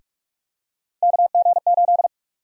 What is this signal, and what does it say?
Morse code, heard as a keyed tone, about 44 wpm, 700 Hz. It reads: KG8
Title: USA Call Sign Prefixes - Repeated 2x - 45wpm